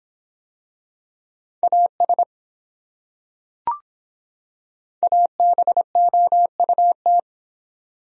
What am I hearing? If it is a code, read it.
Morse code, heard as a keyed tone, about 22 wpm, 700 Hz. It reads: AS ABOUT